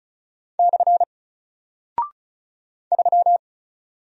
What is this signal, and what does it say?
Morse code, heard as a keyed tone, about 36 wpm, 700 Hz. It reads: / 3
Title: Single Character (Letter:Number:Punctuation:Prosign) Alt 35wpm